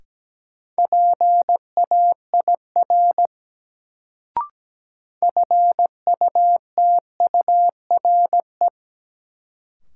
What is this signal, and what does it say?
Morse code, heard as a keyed tone, about 17 wpm, 700 Hz. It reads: PAIR FUTURE
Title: Sets of 3 Words - New Word Drill - 17wpm